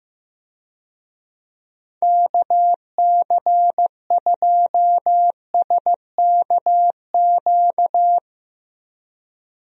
Morse code, heard as a keyed tone, about 15 wpm, 700 Hz. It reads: KC2SKQ
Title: USA Call Signs - 6 Characters - 15wpm